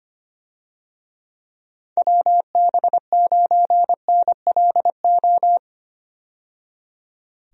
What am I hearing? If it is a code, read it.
Morse code, heard as a keyed tone, about 25 wpm, 700 Hz. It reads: WB9NLO
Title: USA Call Signs - 6 Characters - 25wpm